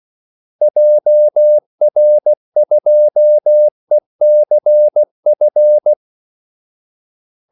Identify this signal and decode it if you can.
Morse code, heard as a keyed tone, about 16 wpm, 600 Hz. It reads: JR2ECF
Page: Kaldesignaler 16 wpm